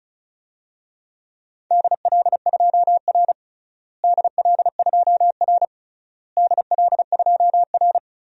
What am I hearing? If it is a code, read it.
Morse code, heard as a keyed tone, about 34 wpm, 700 Hz. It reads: DL2R DL2R DL2R
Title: DXCC-100 Call Signs - 4 Characters - Repeated 3x - 35wpm